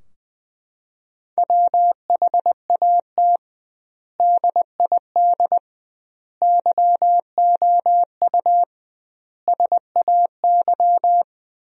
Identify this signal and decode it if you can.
Morse code, heard as a keyed tone, about 21 wpm, 700 Hz. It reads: WHAT DID YOU SAY
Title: 2x Word-Spacing - Sentences from Top 200 Words - 20wpm